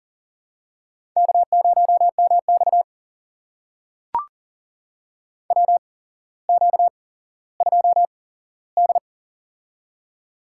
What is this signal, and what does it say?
Morse code, heard as a keyed tone, about 40 wpm, spaced as for 17 wpm, 700 Hz. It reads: K0MX WQ2D
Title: USA Call Signs - 4 Characters - 8x Character Spacing - 40wpm